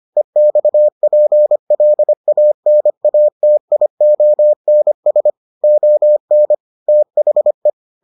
Morse code, heard as a keyed tone, about 25 wpm, 600 Hz. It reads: EXPLANATIONS ON THE